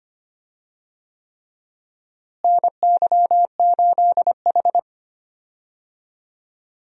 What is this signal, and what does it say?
Morse code, heard as a keyed tone, about 25 wpm, 700 Hz. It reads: NY8H
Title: USA Call Signs - 4 Characters - Repeated 3x - 25wpm